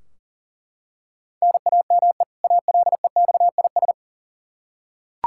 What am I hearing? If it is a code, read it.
Morse code, heard as a keyed tone, about 38 wpm, 700 Hz. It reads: NAME ALEXIS
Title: Practice QSO - Exchange 1 - Beginning - 40wpm